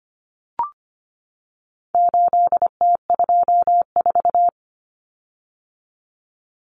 Morse code, heard as a keyed tone, about 25 wpm, 700 Hz. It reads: 8T24
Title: Common Cut Numbers - 4-Digit 25wpm